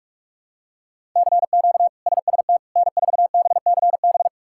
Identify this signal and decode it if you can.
Morse code, heard as a keyed tone, about 41 wpm, 700 Hz. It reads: CQ SST N4BCB
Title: K1USN SST QSO Practice - 45wpm